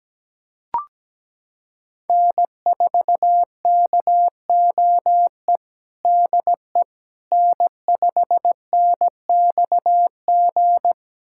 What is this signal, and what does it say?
Morse code, heard as a keyed tone, about 17 wpm, 700 Hz. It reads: N4KOE DE N5NXG
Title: USA Callsign DE Callsign - 5 Characters - 17wpm